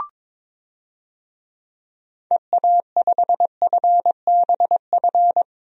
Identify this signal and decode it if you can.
Morse code, heard as a keyed tone, about 22 wpm, 700 Hz. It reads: EA5FBF
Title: DXCC-100 Call Signs - 6 Characters - 22wpm